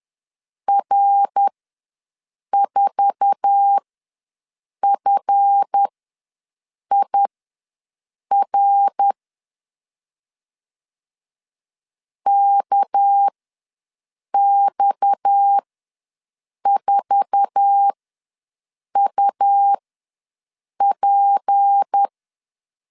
Morse code, heard as a keyed tone, about 10 wpm, spaced as for 6 wpm, 800 Hz. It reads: R4FIR KX4UP
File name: LEZIONI / 8 YJ45 / 8 YJ45 ws 10wpm 3843